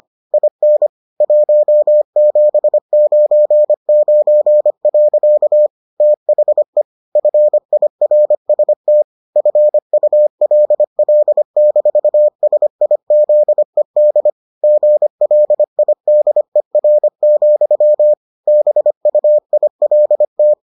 Morse code, heard as a keyed tone, about 25 wpm, 600 Hz. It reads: IN 1799. THE FIRST FULL-SIZED GLIDER, BUILT